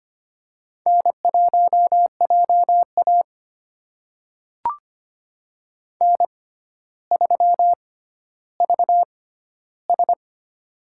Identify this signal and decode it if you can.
Morse code, heard as a keyed tone, about 25 wpm, spaced as for 13 wpm, 700 Hz. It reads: N1JA N3VS
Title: USA Call Signs - 4 Characters - 6x Character Spacing - 25wpm